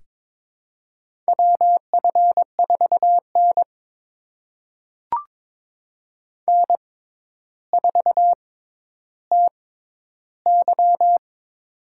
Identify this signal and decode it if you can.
Morse code, heard as a keyed tone, about 22 wpm, spaced as for 11 wpm, 700 Hz. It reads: WF4N N4TY
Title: USA Call Signs - 4 Characters - 6x Character Spacing - 22wpm